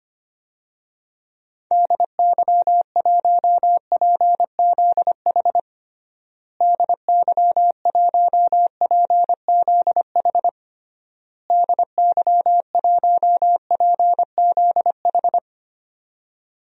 Morse code, heard as a keyed tone, about 25 wpm, 700 Hz. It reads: DY1PZH DY1PZH DY1PZH
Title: DXCC-100 Call Signs - 6 Characters - Repeated 3x - 25wpm